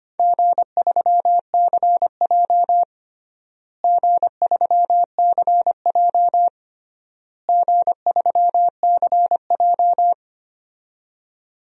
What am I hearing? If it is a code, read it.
Morse code, heard as a keyed tone, about 25 wpm, 700 Hz. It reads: G3CJ G3CJ G3CJ
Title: DXCC-100 Call Signs - 4 Characters - Repeated 3x - 25wpm